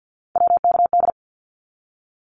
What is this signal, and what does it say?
Morse code, heard as a keyed tone, about 49 wpm, 700 Hz. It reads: WKD
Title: Abbreviations 50wpm